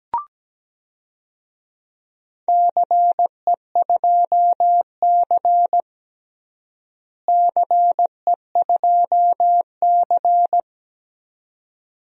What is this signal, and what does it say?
Morse code, heard as a keyed tone, about 17 wpm, 700 Hz. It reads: CE2C CE2C
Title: DXCC-100 Call Signs - 4 Characters - Repeated 2x - 17wpm